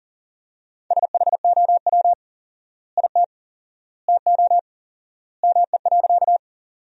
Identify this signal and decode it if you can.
Morse code, heard as a keyed tone, about 40 wpm, 700 Hz. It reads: SHOW IT TO ME.
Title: 4x Word-Spacing - Sentences from Top 200 Words - 40wpm